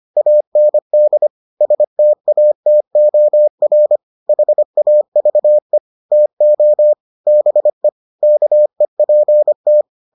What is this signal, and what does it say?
Morse code, heard as a keyed tone, about 25 wpm, 600 Hz. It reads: AND STATOR HAVE TO BE KEPT